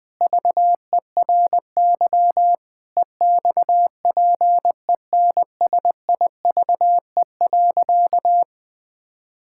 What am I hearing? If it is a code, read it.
Morse code, heard as a keyed tone, about 21 wpm, 700 Hz. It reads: VERY EXPENSIVE.
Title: Sentences from Top 1000 Words - Review Part 6 - 20wpm